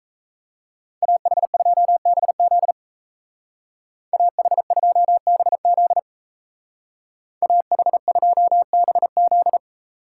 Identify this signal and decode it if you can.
Morse code, heard as a keyed tone, about 37 wpm, 700 Hz. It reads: AH2BZ AH2BZ AH2BZ